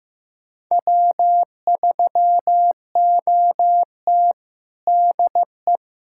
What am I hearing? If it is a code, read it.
Morse code, heard as a keyed tone, about 16 wpm, 700 Hz. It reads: W3OT DE
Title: USA Callsign DE Callsign - 4 Characters - 15wpm